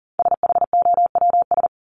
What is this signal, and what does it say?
Morse code, heard as a keyed tone, about 39 wpm, 700 Hz. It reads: SHOWS
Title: ES:S Suffix - Top 100 Words with Base Words 40wpm